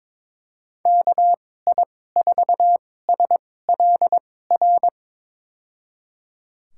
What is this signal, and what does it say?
Morse code, heard as a keyed tone, about 22 wpm, spaced as for 16 wpm, 700 Hz. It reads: KI4SLR
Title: USA Call Signs - 6 Characters - 2x Character Spacing - 22wpm